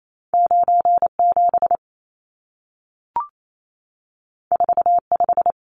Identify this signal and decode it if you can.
Morse code, heard as a keyed tone, about 28 wpm, 700 Hz. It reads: 97 45
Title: Numbers - 2-Digit 28wpm